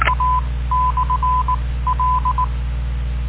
Morse code, heard as a keyed tone, about 19 wpm, spaced as for 15 wpm, 1010 Hz. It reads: A/L